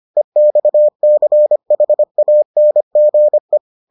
Morse code, heard as a keyed tone, about 25 wpm, 600 Hz. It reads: EXCHANGE